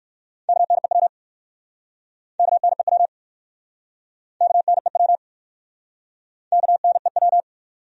Morse code, heard as a keyed tone, about 58 wpm, 700 Hz. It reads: KNEW KNEW KNEW KNEW